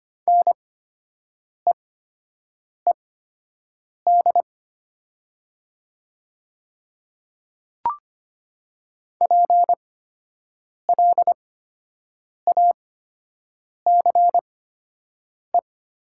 Morse code, heard as a keyed tone, about 25 wpm, spaced as for 7 wpm, 700 Hz. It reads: NEED PLACE
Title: 8x Character Spacing - Top 200 Words - 25wpm